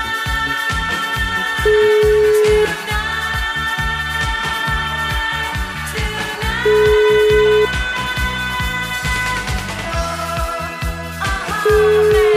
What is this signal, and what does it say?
Ringback tone (Europe, 425 Hz), with noise.